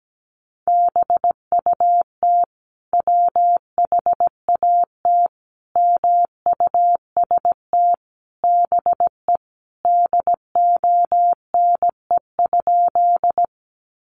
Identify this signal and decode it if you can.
Morse code, heard as a keyed tone, about 17 wpm, 700 Hz. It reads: BUT WHAT MUST BE DONE?
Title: Sentences from Top 500 Words 17wpm